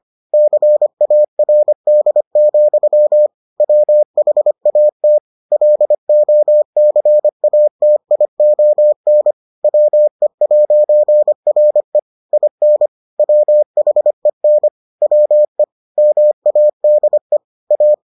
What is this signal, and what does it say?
Morse code, heard as a keyed tone, about 25 wpm, 600 Hz. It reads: CARD, WHAT LOCATION WE'RE IN WHEN WE MADE A